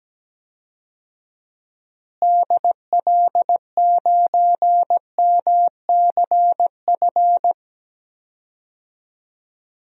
Morse code, heard as a keyed tone, about 17 wpm, 700 Hz. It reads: DL9MCF
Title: DXCC-100 Call Signs - 6 Characters - Repeated 3x - 17wpm